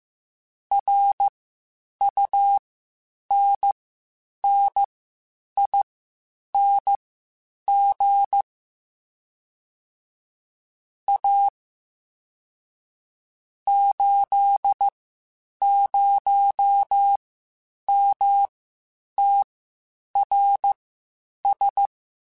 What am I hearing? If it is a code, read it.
Morse code, heard as a keyed tone, about 15 wpm, spaced as for 8 wpm, 800 Hz. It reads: RUNNING A 80MTRS
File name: QSOs / QSOs-mp3s / WD0FBT 1@13